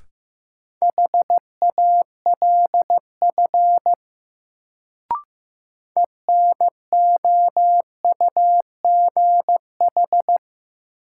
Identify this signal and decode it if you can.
Morse code, heard as a keyed tone, about 15 wpm, 700 Hz. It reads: HALF ENOUGH
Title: Sentences from Top 400 Words - New Word Drill - 15wpm